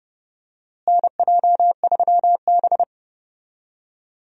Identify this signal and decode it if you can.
Morse code, heard as a keyed tone, about 30 wpm, 700 Hz. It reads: NJ3B
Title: USA Call Signs - 4 Characters - Repeated 2x - 30wpm